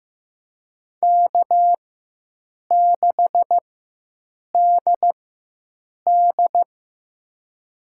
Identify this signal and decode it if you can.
Morse code, heard as a keyed tone, about 15 wpm, spaced as for 7 wpm, 700 Hz. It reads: K6DD
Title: USA Call Signs - 4 Characters - 4x Character Spacing - 15wpm